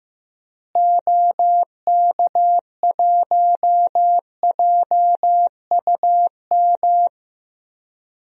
Morse code, heard as a keyed tone, about 16 wpm, 700 Hz. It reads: OK1JUM
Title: DXCC-100 Call Signs - 6 Characters - Repeated 2x - 15wpm